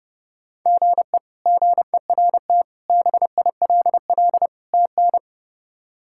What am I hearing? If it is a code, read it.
Morse code, heard as a keyed tone, about 30 wpm, 700 Hz. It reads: GE GERT BILL TN
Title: K1USN SST QSO Practice - 30wpm